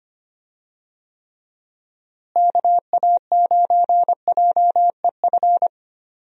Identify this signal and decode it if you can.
Morse code, heard as a keyed tone, about 25 wpm, 700 Hz. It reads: KA9JEF
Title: USA Call Signs - 6 Characters - Repeated 2x - 25wpm